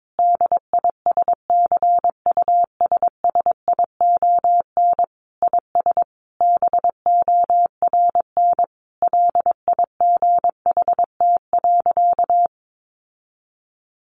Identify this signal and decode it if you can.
Morse code, heard as a keyed tone, about 22 wpm, 700 Hz. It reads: DISCUSSION IS BORN LIGHT.
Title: Sentences from Top 1100 Words 22wpm